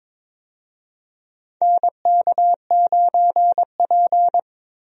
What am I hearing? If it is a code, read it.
Morse code, heard as a keyed tone, about 22 wpm, 700 Hz. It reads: NK9P
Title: USA Call Signs - 4 Characters - Repeated 2x - 22wpm